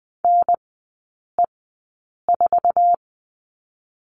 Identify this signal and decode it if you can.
Morse code, heard as a keyed tone, about 20 wpm, spaced as for 8 wpm, 700 Hz. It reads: NE4
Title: 047 Lesson 9e - Number 4 - Three Characters 20wpm